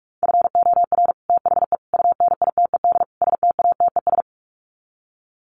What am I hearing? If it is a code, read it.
Morse code, heard as a keyed tone, about 44 wpm, 700 Hz. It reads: FOR THE UNITED STATES
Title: Sets of 4 Words 45wpm